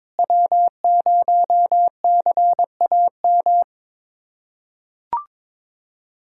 Morse code, heard as a keyed tone, about 22 wpm, 700 Hz. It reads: W0CAM
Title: USA Callsign DE Callsign - 5 Characters - 22wpm